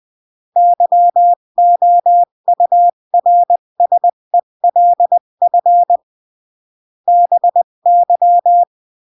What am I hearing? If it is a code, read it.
Morse code, heard as a keyed tone, about 20 wpm, 700 Hz. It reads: YOURSELF BY
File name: CW / jodorowsky / 82 commandments 20-15